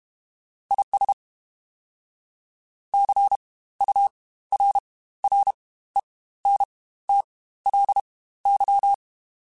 Morse code, heard as a keyed tone, about 32 wpm, spaced as for 16 wpm, 800 Hz. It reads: IS CURRENTLY
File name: Spaced Characters / Spaced Characters-mp3s / Weather5 22 sp @30